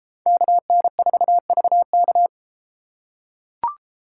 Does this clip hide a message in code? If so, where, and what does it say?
Yes: Morse code from 0.3 to 2.3 s, heard as a keyed tone, about 32 wpm, 700 Hz. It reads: KN4VK